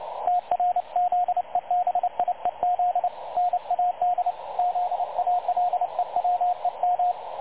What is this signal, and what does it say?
Morse code, heard as a keyed tone, about 28 wpm, 720 Hz, with radio noise. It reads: TRZEBIEZ NAD ZALEWEM
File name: dlazaawansowanych / nagraniazpasma / websdr recording 3565.0kHz